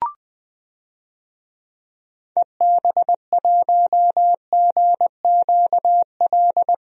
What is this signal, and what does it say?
Morse code, heard as a keyed tone, about 21 wpm, 700 Hz. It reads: EB1GQL
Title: DXCC-100 Call Signs - 6 Characters - 20wpm